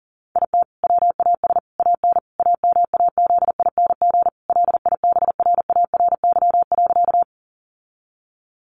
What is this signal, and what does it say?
Morse code, heard as a keyed tone, about 40 wpm, 700 Hz. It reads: IT WAS AN AMAZING LIBRARY.
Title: Sentences from Top 1500 Words 40wpm